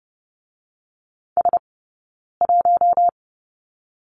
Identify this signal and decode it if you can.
Morse code, heard as a keyed tone, about 30 wpm, 700 Hz. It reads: S 1
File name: course / 30:10wpm / 086 Lesson 15d - Number 5 - Two Characters 30wpm